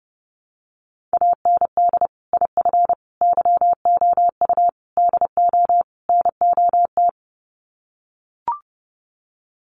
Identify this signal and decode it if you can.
Morse code, heard as a keyed tone, about 30 wpm, 700 Hz. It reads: AND IF YOU DO NOT
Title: Sets of 5 Words 30wpm